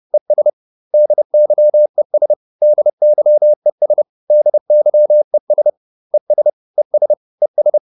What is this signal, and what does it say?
Morse code, heard as a keyed tone, about 30 wpm, 600 Hz. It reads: ES DYES DYES DYES ES ES ES